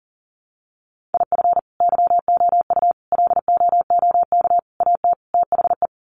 Morse code, heard as a keyed tone, about 39 wpm, 700 Hz. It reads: IF YOU LOOK AT THE
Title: Sets of 5 Words 40wpm